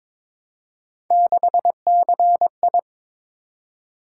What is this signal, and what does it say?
Morse code, heard as a keyed tone, about 22 wpm, 700 Hz. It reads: 6CI
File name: practice / 22wpm / Three Letter-Number Combinations - 22wpm